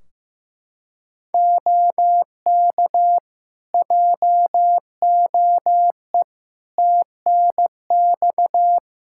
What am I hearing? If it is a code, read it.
Morse code, heard as a keyed tone, about 15 wpm, 700 Hz. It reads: OK JOE TNX
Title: Practice QSO - Exchange 4 - Ending - 15wpm